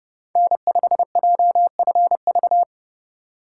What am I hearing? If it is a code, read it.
Morse code, heard as a keyed tone, about 31 wpm, 700 Hz. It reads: N5JFV
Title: USA Call Signs - 5 Characters - 4x Character Spacing - 30wpm